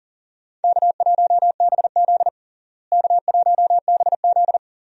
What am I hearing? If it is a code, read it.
Morse code, heard as a keyed tone, about 37 wpm, 700 Hz. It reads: K1BZ K1BZ
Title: USA Call Signs - 4 Characters - Repeated 2x - 40wpm